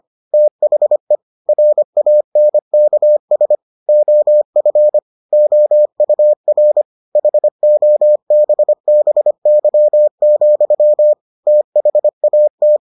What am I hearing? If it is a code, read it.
Morse code, heard as a keyed tone, about 25 wpm, 600 Hz. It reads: THE RANKS OF OUR HOBBY, THAT